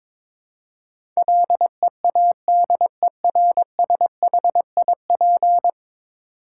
Morse code, heard as a keyed tone, about 21 wpm, 700 Hz. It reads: LEADERSHIP
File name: practice / 22wpm / 10 Letter Words - First 3 Letters Given - 22wpm